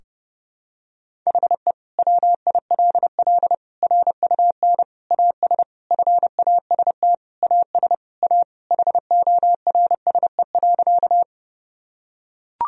Morse code, heard as a keyed tone, about 30 wpm, 700 Hz. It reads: HE WILL RUN AS FAST AS A HORSE.
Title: Sentences from Top 1000 Words - Review Part 2 - 30wpm